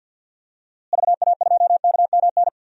Morse code, heard as a keyed tone, about 52 wpm, 700 Hz. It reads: VA2KMN
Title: DXCC-100 Call Signs - 6 Characters - Repeated 3x - 50wpm